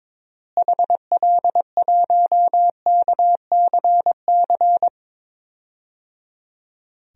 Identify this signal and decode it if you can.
Morse code, heard as a keyed tone, about 21 wpm, 700 Hz. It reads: HL1KCC
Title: DXCC-100 Call Signs - 6 Characters - Repeated 3x - 22wpm